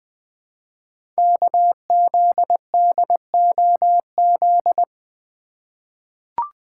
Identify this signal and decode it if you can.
Morse code, heard as a keyed tone, about 20 wpm, 700 Hz. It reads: KZDOZ
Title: Code Groups - 5 Letter:Numbers - 20wpm